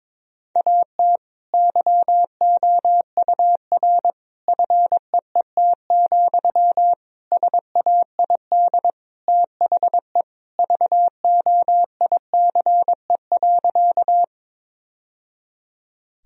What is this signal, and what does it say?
Morse code, heard as a keyed tone, about 22 wpm, 700 Hz. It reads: AT YOUR FEET, SAID THE VOICE.
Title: Sentences from Top 1000 Words - Review Part 1 - 22wpm